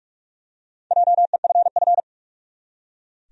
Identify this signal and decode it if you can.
Morse code, heard as a keyed tone, about 46 wpm, 700 Hz. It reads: JEFF
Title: USA Common Names - Max 4 Letters - 45wpm